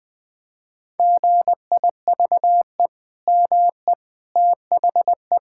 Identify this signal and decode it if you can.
Morse code, heard as a keyed tone, about 21 wpm, 700 Hz. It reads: GIVE ME THE
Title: Sentences from Top 1000 Words - Review Part 6 - 20wpm